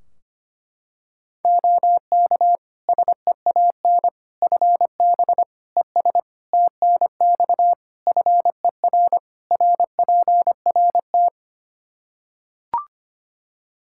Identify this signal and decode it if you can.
Morse code, heard as a keyed tone, about 25 wpm, 700 Hz. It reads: OK SEAN FB ES TNX FER RPRT
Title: Practice QSO - Exchange 2 - Info - 25wpm